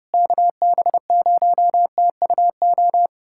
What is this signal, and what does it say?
Morse code, heard as a keyed tone, about 32 wpm, 700 Hz. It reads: KB0TUO
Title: USA Callsign DE Callsign 30wpm